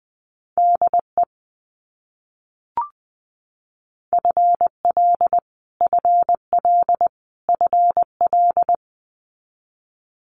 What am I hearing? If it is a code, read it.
Morse code, heard as a keyed tone, about 20 wpm, 700 Hz. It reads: DE FL FL FL
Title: US States Repeated 3x 20wpm